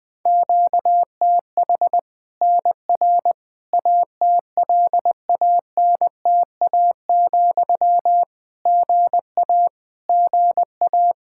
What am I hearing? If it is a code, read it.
Morse code, heard as a keyed tone, about 20 wpm, 700 Hz. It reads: QTH NR ATLANTA, GA GA
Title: K1USN SST QSO Practice 20wpm